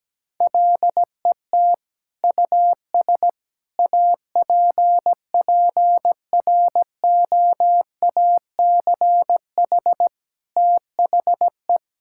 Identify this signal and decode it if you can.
Morse code, heard as a keyed tone, about 17 wpm, 700 Hz. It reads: LET US APPROACH THE
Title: Sentences from Top 1000 Words - Review Part 6 - 17wpm